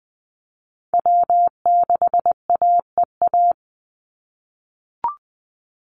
Morse code, heard as a keyed tone, about 20 wpm, 700 Hz. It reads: W6AEA
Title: Call Signs - 20wpm